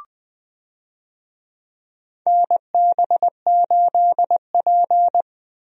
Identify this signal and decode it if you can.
Morse code, heard as a keyed tone, about 20 wpm, 700 Hz. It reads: NB8P
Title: USA Call Signs - 4 Characters - 20wpm